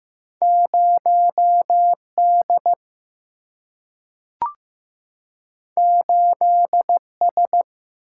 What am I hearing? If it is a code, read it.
Morse code, heard as a keyed tone, about 15 wpm, 700 Hz. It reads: 0D 8S
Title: Single-Letter Single-Number 15wpm